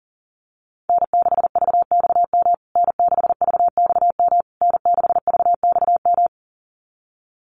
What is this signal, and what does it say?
Morse code, heard as a keyed tone, about 40 wpm, 700 Hz. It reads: N6VXM N6VXM N6VXM
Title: USA Call Signs Repeated 3x 40wpm